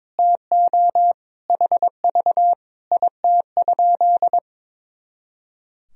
Morse code, heard as a keyed tone, about 22 wpm, 700 Hz. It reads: TO HV IT?
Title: Sentences from Top 100 Words and Abbreviations 22wpm